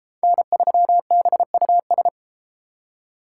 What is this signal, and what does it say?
Morse code, heard as a keyed tone, about 31 wpm, 700 Hz. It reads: N3BUS